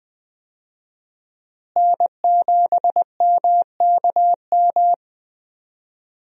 Morse code, heard as a keyed tone, about 20 wpm, 700 Hz. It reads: N7MKM
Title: USA Call Signs - 5 Characters - Repeated 2x - 20wpm